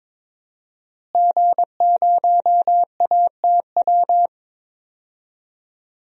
Morse code, heard as a keyed tone, about 22 wpm, 700 Hz. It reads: G0ATW
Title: DXCC-100 Call Signs - 5 Characters - Repeated 2x - 22wpm